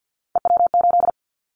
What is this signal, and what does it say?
Morse code, heard as a keyed tone, about 50 wpm, 700 Hz. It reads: EW8
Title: Callsign Prefixes for Top 50 DXCC - 50wpm